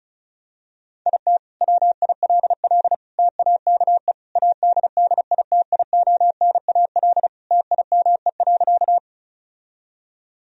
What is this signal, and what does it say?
Morse code, heard as a keyed tone, about 34 wpm, 700 Hz. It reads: IT WILL TAKE ADDITIONAL TIME.
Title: Sentences from Top 1000 Words - Review Part 4 - 35wpm